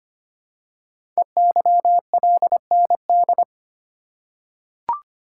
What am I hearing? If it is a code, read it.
Morse code, heard as a keyed tone, about 25 wpm, 700 Hz. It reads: EYLND
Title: Code Groups - 5 Letters - 25wpm